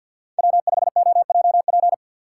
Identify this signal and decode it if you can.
Morse code, heard as a keyed tone, about 53 wpm, 690 Hz. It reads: W5OJP
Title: USA Call Signs - 5 Characters - 3x Character Spacing - 50wpm